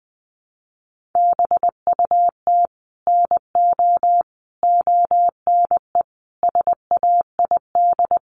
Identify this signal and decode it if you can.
Morse code, heard as a keyed tone, about 20 wpm, 700 Hz. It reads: BUT NO ONE SAID
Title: Sentences from Top 1000 Words 20wpm